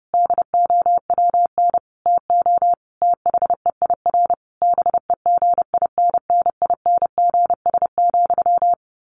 Morse code, heard as a keyed tone, about 30 wpm, 700 Hz. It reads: DOWN TO THEIR BEGINNINGS,
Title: A Sound Of Thunder 30wpm